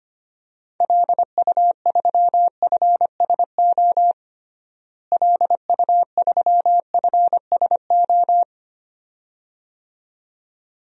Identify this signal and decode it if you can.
Morse code, heard as a keyed tone, about 25 wpm, 700 Hz. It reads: LU3FSO LU3FSO
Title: DXCC-100 Call Signs - 6 Characters - Repeated 2x - 25wpm